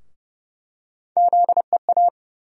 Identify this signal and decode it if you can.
Morse code, heard as a keyed tone, about 30 wpm, 700 Hz. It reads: ZEA
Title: USA Call Sign Postfixes - 3 Letters 30wpm